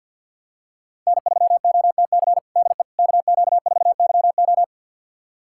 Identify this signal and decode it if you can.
Morse code, heard as a keyed tone, about 51 wpm, 700 Hz. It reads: N3OTC DE KQ4YO
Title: USA Callsign DE Callsign - 5 Characters - 50wpm